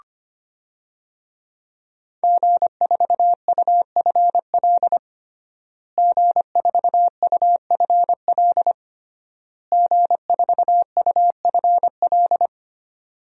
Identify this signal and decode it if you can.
Morse code, heard as a keyed tone, about 25 wpm, 700 Hz. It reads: G4UFL G4UFL G4UFL
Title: DXCC-100 Call Signs - 5 Characters - Repeated 3x - 25wpm